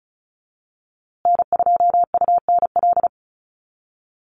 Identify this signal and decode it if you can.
Morse code, heard as a keyed tone, about 34 wpm, 700 Hz. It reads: N2UNL
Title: USA Call Signs Repeated 2x 35wpm